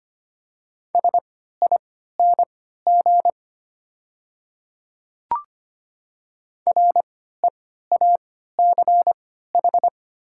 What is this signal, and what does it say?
Morse code, heard as a keyed tone, about 25 wpm, spaced as for 14 wpm, 700 Hz. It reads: SING REACH
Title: 3x Character Spacing - Top 500 Words - 25wpm